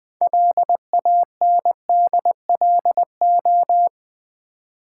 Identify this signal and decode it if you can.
Morse code, heard as a keyed tone, about 20 wpm, 700 Hz. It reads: LANDLO
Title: Build A Word - 2 Letters At A Time - 8-Letter Words - 20wpm